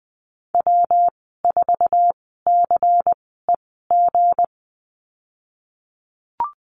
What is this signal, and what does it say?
Morse code, heard as a keyed tone, about 19 wpm, spaced as for 14 wpm, 700 Hz. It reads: W4CEG
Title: 161 Lesson 26g - Letter g - Call Signs 20wpm